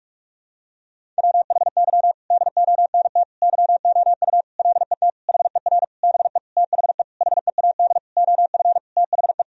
Morse code, heard as a keyed tone, about 42 wpm, 700 Hz. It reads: WHY DONT YOU LET HER BE THE HEAD OF THE
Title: Sentences from Top 1000 Words - Review Part 1 - 45wpm